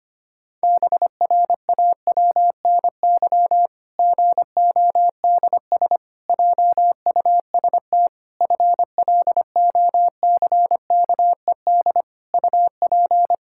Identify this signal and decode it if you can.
Morse code, heard as a keyed tone, about 25 wpm, 700 Hz. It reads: BRAWNY GODS JUST FLOCKED UP
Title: Pangram Challenge v2 - 25wpm